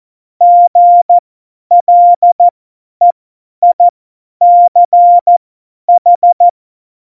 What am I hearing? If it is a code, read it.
Morse code, heard as a keyed tone, about 14 wpm, spaced as for 10 wpm, 700 Hz. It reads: GLEICH
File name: Morsekurs / Witze / Witz 12. Februar 2019, 50 F70